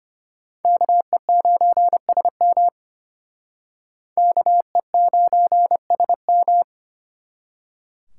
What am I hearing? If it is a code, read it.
Morse code, heard as a keyed tone, about 31 wpm, 700 Hz. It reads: KE9SM KE9SM
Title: Speed Racing - USA 2x2 Call Signs 25wpm